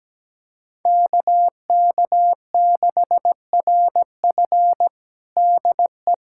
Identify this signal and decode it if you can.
Morse code, heard as a keyed tone, about 17 wpm, 700 Hz. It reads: KK6RF DE
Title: USA Callsign DE Callsign - 5 Characters - 17wpm